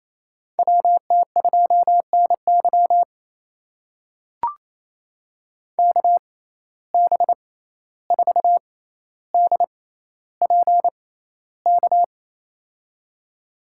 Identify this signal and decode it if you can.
Morse code, heard as a keyed tone, about 27 wpm, spaced as for 14 wpm, 700 Hz. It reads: WT2NY KB4DPK
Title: USA Call Signs - 6x Character Spacing - 28wpm